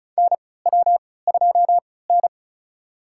Morse code, heard as a keyed tone, about 36 wpm, spaced as for 20 wpm, 700 Hz. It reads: NW2N
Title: USA Call Signs - 3x Character Spacing - 35wpm